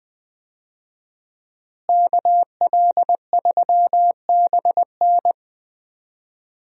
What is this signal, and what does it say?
Morse code, heard as a keyed tone, about 20 wpm, 700 Hz. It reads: KL3BN
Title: USA Call Signs - 5 Characters - Repeated 3x - 20wpm